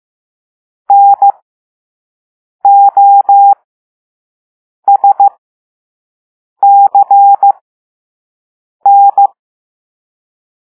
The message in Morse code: NOSCN